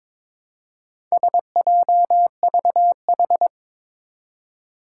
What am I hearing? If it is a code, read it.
Morse code, heard as a keyed tone, about 22 wpm, 700 Hz. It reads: SJVH
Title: Four Challenging Letter-Number Combinations 22wpm